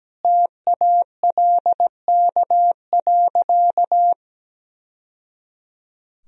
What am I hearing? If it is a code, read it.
Morse code, heard as a keyed tone, about 17 wpm, 700 Hz. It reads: TALK.
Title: Sentences from Top 1000 Words - Review Part 4 - 17wpm